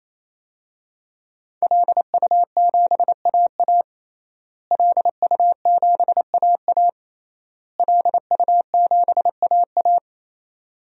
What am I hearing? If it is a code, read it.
Morse code, heard as a keyed tone, about 27 wpm, 700 Hz. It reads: LU7AA LU7AA LU7AA
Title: DXCC-100 Call Signs - 5 Characters - Repeated 3x - 28wpm